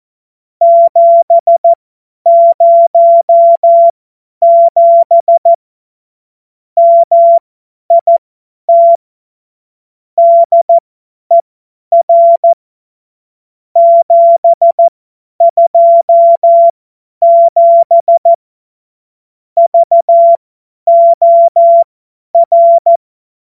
Morse code, heard as a keyed tone, about 14 wpm, spaced as for 10 wpm, 680 Hz. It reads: 707 MIT DER 727 VOR
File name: Morsekurs / Witze / Witz 25. Juli 2023 50 F70